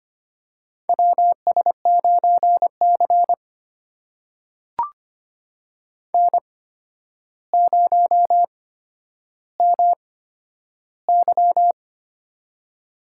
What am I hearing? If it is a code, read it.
Morse code, heard as a keyed tone, about 25 wpm, spaced as for 11 wpm, 700 Hz. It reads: WS9C N0MY
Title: USA Call Signs - 4 Characters - 8x Character Spacing - 25wpm